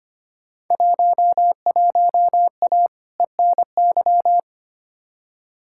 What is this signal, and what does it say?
Morse code, heard as a keyed tone, about 25 wpm, 700 Hz. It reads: 11A ENY
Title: Field Day Exchanges Encore - 25wpm